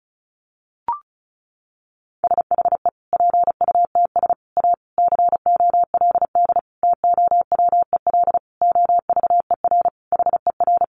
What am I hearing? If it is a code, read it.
Morse code, heard as a keyed tone, about 34 wpm, 700 Hz. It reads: SHE PUTS A COLD TOWEL OVER HER
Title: Easy Phrases 35wpm